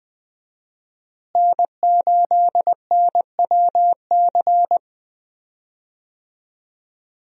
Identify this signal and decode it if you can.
Morse code, heard as a keyed tone, about 20 wpm, 700 Hz. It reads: N8NWC
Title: USA Call Signs - 5 Characters - Repeated 3x - 20wpm